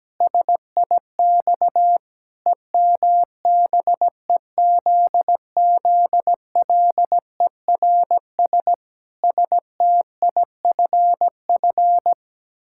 Morse code, heard as a keyed tone, about 17 wpm, 700 Hz. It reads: SIX EMBEZZLERS STIFF
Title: Pangram Challenge v2 - 17wpm